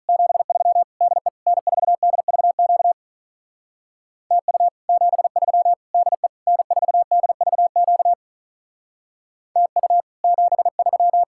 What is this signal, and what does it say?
Morse code, heard as a keyed tone, about 44 wpm, 700 Hz. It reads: 73 DE N4DVQ TU 73 DE N4DVQ TU 73